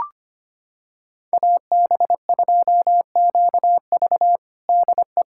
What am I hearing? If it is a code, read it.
Morse code, heard as a keyed tone, about 25 wpm, 700 Hz. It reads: AB2QV DE
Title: USA Callsign DE Callsign - 5 Characters - 25wpm